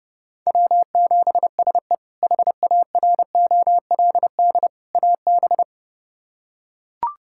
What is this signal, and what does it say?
Morse code, heard as a keyed tone, about 32 wpm, 700 Hz. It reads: W7SE HAROLD AB
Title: K1USN SST QSO Practice - 30wpm